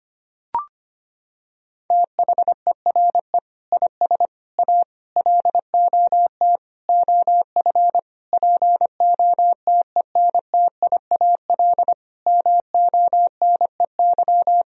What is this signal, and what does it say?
Morse code, heard as a keyed tone, about 25 wpm, 700 Hz. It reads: THERE IS A LOT OF POTENTIAL MONEY
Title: Sentences from Top 1000 Words - Review Part 4 - 25wpm